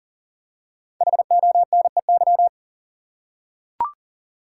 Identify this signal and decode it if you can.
Morse code, heard as a keyed tone, about 40 wpm, 700 Hz. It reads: HONEY
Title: Five Letter Words - 40wpm